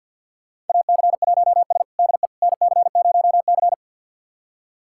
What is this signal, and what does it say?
Morse code, heard as a keyed tone, about 50 wpm, 690 Hz. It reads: AC1S DE NC0C
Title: USA Callsign DE Callsign - 4 Characters - 50wpm